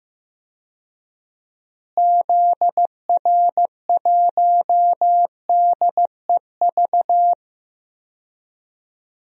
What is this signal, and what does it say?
Morse code, heard as a keyed tone, about 16 wpm, 700 Hz. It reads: ZR1DEV
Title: DXCC-100 Call Signs - 6 Characters - Repeated 3x - 15wpm